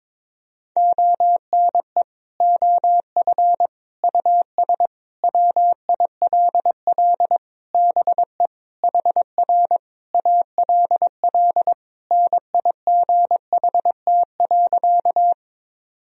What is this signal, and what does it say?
Morse code, heard as a keyed tone, about 22 wpm, 700 Hz. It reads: ONE OF US WILL BE HR ALL NIGHT.
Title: Sentences from Top 300 Words and Abbreviations 22wpm